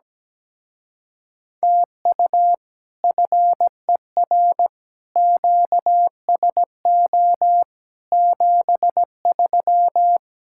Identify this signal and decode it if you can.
Morse code, heard as a keyed tone, about 17 wpm, 700 Hz. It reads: TU FER QSO 73
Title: Speed-Racing Common QSO Phrases Encore - 17wpm